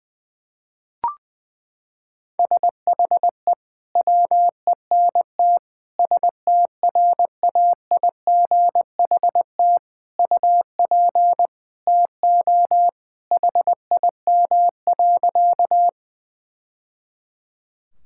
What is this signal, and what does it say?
Morse code, heard as a keyed tone, about 20 wpm, 700 Hz. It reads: SHE WENT STRAIGHT UP TO HIM.
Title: Sentences from Top 1000 Words - Review Part 6 - 20wpm